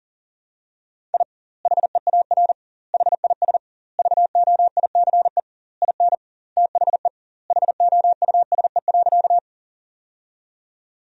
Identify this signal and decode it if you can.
Morse code, heard as a keyed tone, about 40 wpm, 700 Hz. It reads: I HEAR HIS VOICE IN THE HOUSE.
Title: 2x Word-Spacing - Sentences from Top 500 Words - 40wpm